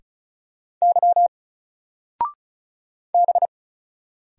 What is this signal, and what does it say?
Morse code, heard as a keyed tone, about 36 wpm, 700 Hz. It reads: Y B
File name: Single-Letter-Number-35wpm